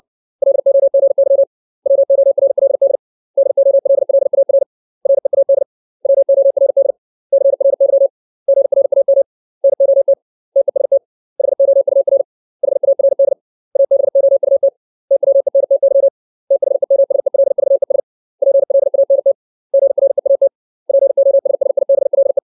Morse code, heard as a keyed tone, about 60 wpm, 550 Hz. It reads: LOGY WORLD DOLLAR READ WORD CRY GRAM NOT TEST HOUR HARD ABOUT TWENTY THEMSELVES PLANT GREAT POSSIBLE
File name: 632 words / 632 Words 60WPM